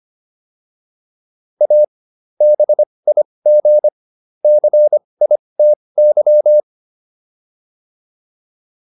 Morse code, heard as a keyed tone, about 25 wpm, spaced as for 20 wpm, 600 Hz. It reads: A BIG CITY